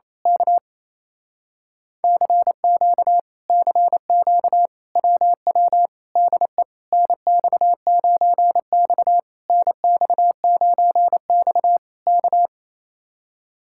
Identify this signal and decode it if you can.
Morse code, heard as a keyed tone, about 27 wpm, 700 Hz. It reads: K CQ CQ WW DE NX9X NX9X K
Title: Speed-Racing Common QSO Phrases - 28wpm